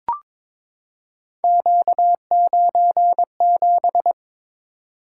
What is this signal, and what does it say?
Morse code, heard as a keyed tone, about 22 wpm, 700 Hz. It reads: Q97
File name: practice / 22wpm / Three Letter-Number Combinations - 22wpm